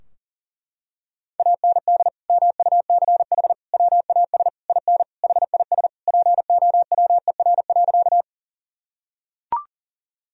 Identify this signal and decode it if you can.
Morse code, heard as a keyed tone, about 37 wpm, 700 Hz. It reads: AND MUCH WAS IN HIS POWER.
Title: Sentences from Top 1000 Words - Review Part 2 - 40wpm